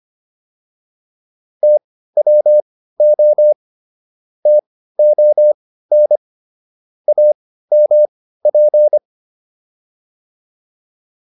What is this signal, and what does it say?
Morse code, heard as a keyed tone, about 25 wpm, spaced as for 15 wpm, 600 Hz. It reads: TWO TON AMP